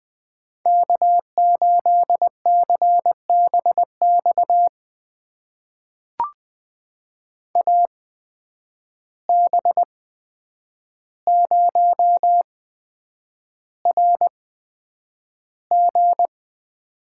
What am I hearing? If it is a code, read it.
Morse code, heard as a keyed tone, about 20 wpm, spaced as for 9 wpm, 700 Hz. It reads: K8CBX AB0RG
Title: USA Call Signs - 5 Characters - 8x Character Spacing - 20wpm